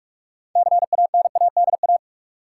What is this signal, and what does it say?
Morse code, heard as a keyed tone, about 44 wpm, 700 Hz. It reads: CANADA
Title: Sets of 5 Words - New Word Drill - 45wpm